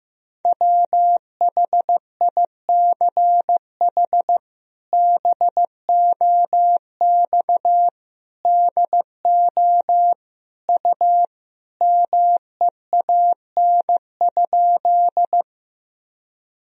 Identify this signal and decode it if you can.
Morse code, heard as a keyed tone, about 15 wpm, 700 Hz. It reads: WHICH BOX DO U MEAN?
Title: Sentences from Top 500 Words and Abbreviations 15wpm